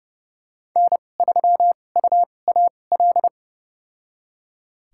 Morse code, heard as a keyed tone, about 30 wpm, spaced as for 22 wpm, 700 Hz. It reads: N3UAL
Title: USA Call Signs - 5 Characters - 2x Character Spacing - 30wpm